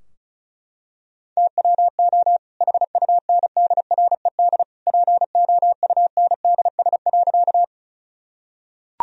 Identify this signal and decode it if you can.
Morse code, heard as a keyed tone, about 34 wpm, 700 Hz. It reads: TWO HUNDRED POUNDS.
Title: Sentences from Top 1000 Words - Review Part 2 - 35wpm